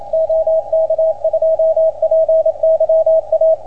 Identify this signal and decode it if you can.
Morse code, heard as a keyed tone, about 28 wpm, 660 Hz, with radio noise. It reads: OK2PYA